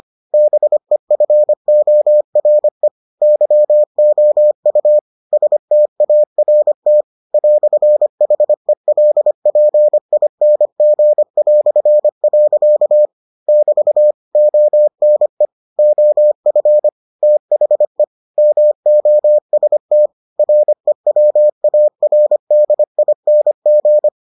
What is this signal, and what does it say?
Morse code, heard as a keyed tone, about 25 wpm, 600 Hz. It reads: BEFORE YOU START "HELPING". <BT> ONE OF THE MOST REWARDING